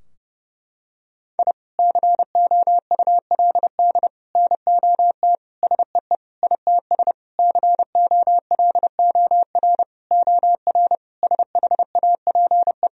Morse code, heard as a keyed tone, about 32 wpm, 700 Hz. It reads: I COULD NOT SEE ITS COLOR OR SHAPE
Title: Sentences from Top 1000 Words - Review Part 3 - 30wpm